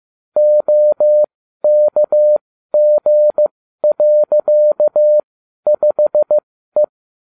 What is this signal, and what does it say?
Morse code, heard as a keyed tone, about 15 wpm, spaced as for 12 wpm, 600 Hz. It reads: OKG.5E